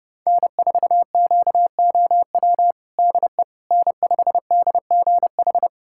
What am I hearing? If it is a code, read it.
Morse code, heard as a keyed tone, about 30 wpm, 700 Hz. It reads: N4QOW DE N5DGH
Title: USA Callsign DE Callsign - 5 Characters - 30wpm